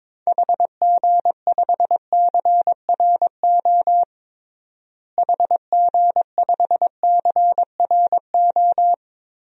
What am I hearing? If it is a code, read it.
Morse code, heard as a keyed tone, about 22 wpm, 700 Hz. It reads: HG5CRO HG5CRO
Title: DXCC-100 Call Signs - 6 Characters - Repeated 3x - 22wpm